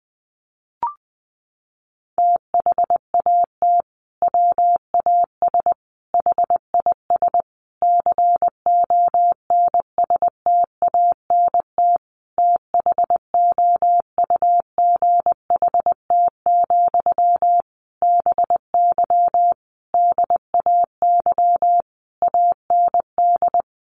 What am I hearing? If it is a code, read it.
Morse code, heard as a keyed tone, about 20 wpm, 700 Hz. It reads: THAT WAS HIS CONSTANT THOUGHT, BY DAY AND
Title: Sentences from Top 1500 Words 20wpm